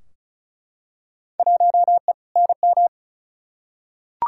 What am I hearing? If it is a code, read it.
Morse code, heard as a keyed tone, about 36 wpm, 700 Hz. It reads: 1E NM
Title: Field Day Exchanges - 35wpm